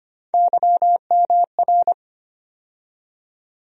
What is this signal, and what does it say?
Morse code, heard as a keyed tone, about 25 wpm, 700 Hz. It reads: YMR
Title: USA Call Sign Postfixes - 3 Letters 25wpm